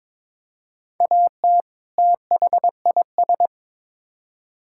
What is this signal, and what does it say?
Morse code, heard as a keyed tone, about 22 wpm, 700 Hz. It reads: AT THIS
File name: Build A Phrase - 2-Word Phrases - 22wpm